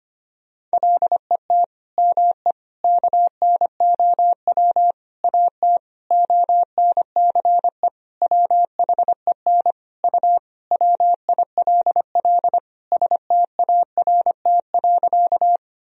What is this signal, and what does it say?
Morse code, heard as a keyed tone, about 25 wpm, 700 Hz. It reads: LET ME KNOW AT ONCE WHEN U WILL START.
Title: Sentences from Top 300 Words and Abbreviations 25wpm